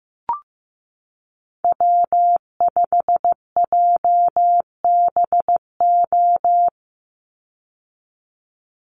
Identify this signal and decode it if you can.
Morse code, heard as a keyed tone, about 15 wpm, 700 Hz. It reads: W5JBO
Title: Call Signs Repeated 3x - 15wpm